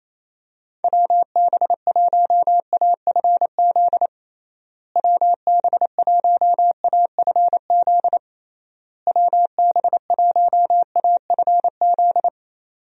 Morse code, heard as a keyed tone, about 28 wpm, 700 Hz. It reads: WB1AFZ WB1AFZ WB1AFZ
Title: USA Call Signs - 6 Characters - Repeated 3x - 28wpm